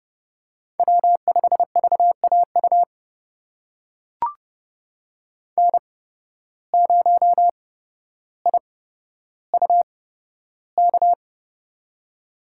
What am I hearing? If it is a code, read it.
Morse code, heard as a keyed tone, about 30 wpm, spaced as for 13 wpm, 700 Hz. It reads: W5VAU N0IUK
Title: USA Call Signs - 5 Characters - 8x Character Spacing - 30wpm